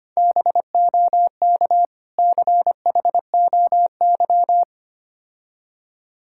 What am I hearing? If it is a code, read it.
Morse code, heard as a keyed tone, about 25 wpm, 700 Hz. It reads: BOK CHOY
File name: practice / 25wpm / Difficult Words with Hints - 25wpm